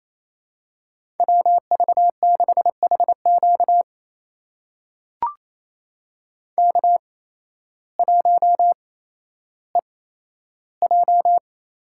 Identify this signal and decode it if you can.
Morse code, heard as a keyed tone, about 28 wpm, spaced as for 13 wpm, 700 Hz. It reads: WV6HQ K1EJ
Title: USA Call Signs - 5 Characters - 8x Character Spacing - 28wpm